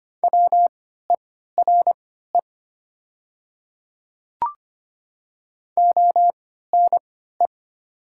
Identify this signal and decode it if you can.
Morse code, heard as a keyed tone, about 25 wpm, spaced as for 14 wpm, 700 Hz. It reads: WERE ONE
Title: 3x Character Spacing - Top 100 Words - 25wpm